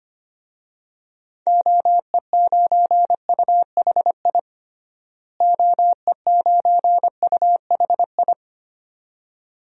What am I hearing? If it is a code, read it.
Morse code, heard as a keyed tone, about 25 wpm, 700 Hz. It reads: OE9UHI OE9UHI
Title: DXCC-100 Call Signs - 6 Characters - Repeated 2x - 25wpm